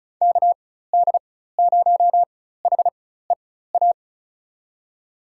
Morse code, heard as a keyed tone, about 34 wpm, spaced as for 16 wpm, 700 Hz. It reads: KD0HEA
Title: USA Call Signs - 6 Characters - 4x Character Spacing - 35wpm